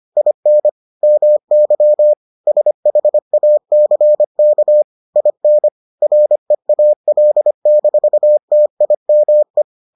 Morse code, heard as a keyed tone, about 25 wpm, 600 Hz. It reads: IN MY SHACK IN REAL-TIME